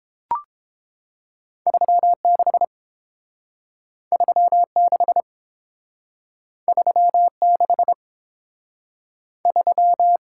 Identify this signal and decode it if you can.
Morse code, heard as a keyed tone, about 31 wpm, 700 Hz. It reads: 36 36 36 3